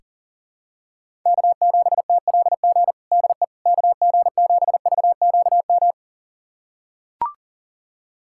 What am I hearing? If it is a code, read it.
Morse code, heard as a keyed tone, about 40 wpm, 700 Hz. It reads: K7TLG DE KG7VQM
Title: USA Callsign DE Callsign 40wpm